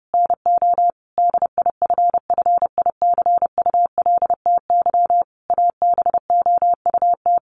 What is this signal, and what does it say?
Morse code, heard as a keyed tone, about 30 wpm, 700 Hz. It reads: NO DIFFICULTY ABOUT
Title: Sentences from Top 1500 Words 30wpm